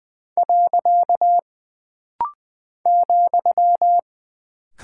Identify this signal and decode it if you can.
Morse code, heard as a keyed tone, about 20 wpm, 700 Hz. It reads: . ,
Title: Single Punctuation-Prosign - Rapid-Fire - 20wpm